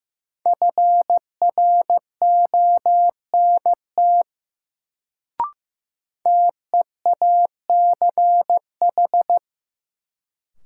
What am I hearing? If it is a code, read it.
Morse code, heard as a keyed tone, about 15 wpm, 700 Hz. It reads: FRONT TEACH
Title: Sentences from Top 500 Words - New Word Drill - 15wpm